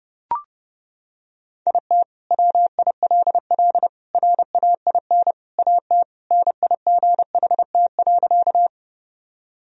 Morse code, heard as a keyed tone, about 30 wpm, 700 Hz. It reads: IT WILL RAIN AT NIGHT.
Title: Sentences from Top 1000 Words - Review Part 1 - 30wpm